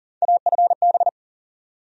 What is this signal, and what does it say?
Morse code, heard as a keyed tone, about 40 wpm, 700 Hz. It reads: AFB
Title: USA Call Sign Postfixes - 3 Letters 40wpm